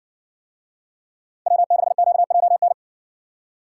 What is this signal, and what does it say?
Morse code, heard as a keyed tone, about 55 wpm, 700 Hz. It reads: W6QJN